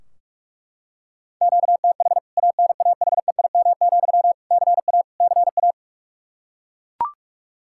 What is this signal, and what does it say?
Morse code, heard as a keyed tone, about 41 wpm, 700 Hz. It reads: QTH ANAHEIM, CA CA
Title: K1USN SST QSO Practice 45wpm